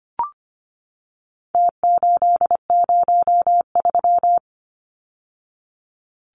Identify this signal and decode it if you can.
Morse code, heard as a keyed tone, about 25 wpm, 700 Hz. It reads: T803
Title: Numbers - Contest Style - 4-Digit 25wpm